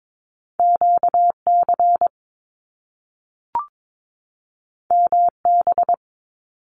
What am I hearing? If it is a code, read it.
Morse code, heard as a keyed tone, about 22 wpm, 700 Hz. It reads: QC MB
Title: Canadian Provinces and Territories 22wpm